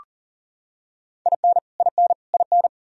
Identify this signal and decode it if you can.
Morse code, heard as a keyed tone, about 40 wpm, 700 Hz. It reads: IN IN IN
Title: Top 100 Words Repeated 40wpm